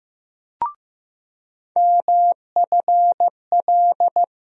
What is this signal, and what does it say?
Morse code, heard as a keyed tone, about 15 wpm, 700 Hz. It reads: MFL
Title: USA Call Sign Postfixes - 3 Letters 15wpm